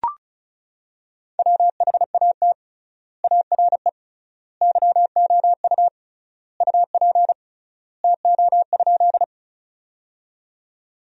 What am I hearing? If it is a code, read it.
Morse code, heard as a keyed tone, about 36 wpm, 700 Hz. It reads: WHAT ARE YOU UP TO?
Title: 3x Word-Spacing - Sentences from Top 100 Words - 35wpm